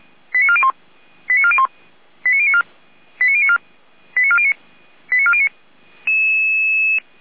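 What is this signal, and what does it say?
Five-tone selective call (ZVEI), heard as quick runs of five tones, tones of about 80 ms. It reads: 89401 89401 89094 89094 89409 89409
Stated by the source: ZVEI five-tone selective call received via SDR.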